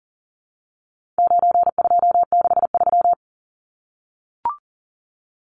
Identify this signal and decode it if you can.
Morse code, heard as a keyed tone, about 40 wpm, 700 Hz. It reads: 9263
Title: Numbers - 4-Digit 40wpm